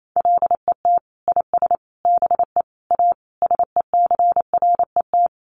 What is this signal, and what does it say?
Morse code, heard as a keyed tone, about 28 wpm, 700 Hz. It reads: LET IS BE A SECRET
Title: Sentences from Top 1200 Words 28wpm